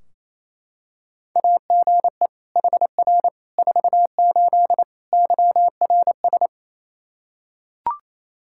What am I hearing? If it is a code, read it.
Morse code, heard as a keyed tone, about 28 wpm, 700 Hz. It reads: AGE HR 48 YRS
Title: Practice QSO - Exchange 3 - More Info - 28wpm